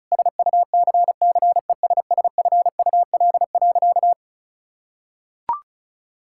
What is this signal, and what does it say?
Morse code, heard as a keyed tone, about 34 wpm, 700 Hz. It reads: SUCCESSFUL.
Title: Sentences from Top 1000 Words - Review Part 6 - 35wpm